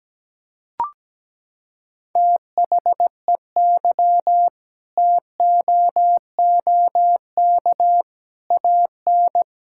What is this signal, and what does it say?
Morse code, heard as a keyed tone, about 17 wpm, 700 Hz. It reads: THEY TOOK AN
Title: Sentences from Top 1000 Words - Review Part 6 - 17wpm